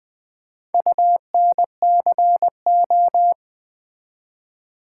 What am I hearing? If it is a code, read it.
Morse code, heard as a keyed tone, about 21 wpm, 700 Hz. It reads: UNCO
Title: Build A Word - 4 Letters At A Time - 11-Letter Words - 20wpm